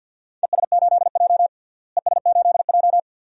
Morse code, heard as a keyed tone, about 51 wpm, 700 Hz. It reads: ES8J ES8J
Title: DXCC-100 Call Signs - 4 Characters - Repeated 2x - 50wpm